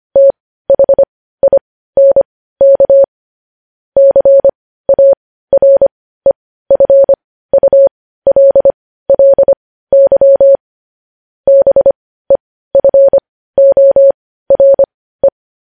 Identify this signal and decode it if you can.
Morse code, heard as a keyed tone, about 25 wpm, spaced as for 15 wpm, 550 Hz. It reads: THINK CAREFULLY BEFORE
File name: podcast / News headlines-1x-25x15